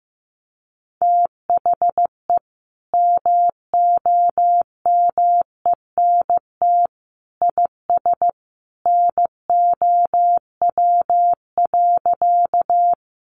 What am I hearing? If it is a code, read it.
Morse code, heard as a keyed tone, about 15 wpm, 700 Hz. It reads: THE MOMENT IS NOW.
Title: Sentences from Top 1000 Words 15wpm